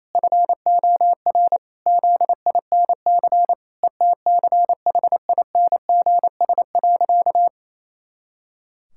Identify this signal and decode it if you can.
Morse code, heard as a keyed tone, about 28 wpm, 700 Hz. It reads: FOR ZINC ETCHINGS.
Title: Pangram Challenge v2 - 28wpm